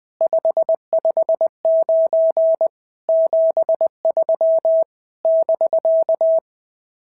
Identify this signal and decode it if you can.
Morse code, heard as a keyed tone, about 21 wpm, 690 Hz. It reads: 559 73 <BK>
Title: SOTA QSO Practice 20wpm